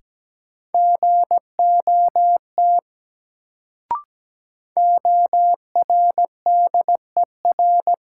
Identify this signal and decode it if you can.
Morse code, heard as a keyed tone, about 17 wpm, 700 Hz. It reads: GOT ORDER
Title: Sets of 2 Words - New Word Drill - 17wpm